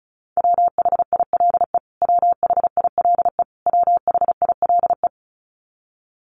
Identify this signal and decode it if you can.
Morse code, heard as a keyed tone, about 34 wpm, 700 Hz. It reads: WHILE WHILE WHILE
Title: Top 300 Words Repeated 35wpm